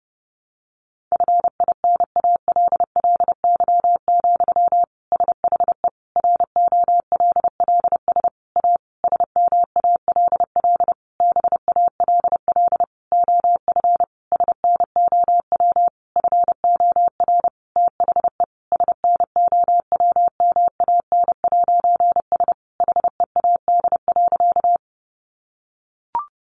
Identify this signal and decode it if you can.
Morse code, heard as a keyed tone, about 30 wpm, 700 Hz. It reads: FINALLY, SHE ROLLS A SMALL BALL OF SNOW FOR THE SNOWMAN'S HEAD.
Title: Easy Phrases 30wpm